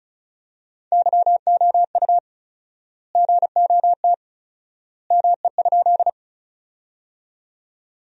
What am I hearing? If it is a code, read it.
Morse code, heard as a keyed tone, about 34 wpm, 700 Hz. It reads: YOU GOT ME?
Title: 4x Word-Spacing - Sentences from Top 300 Words - 35wpm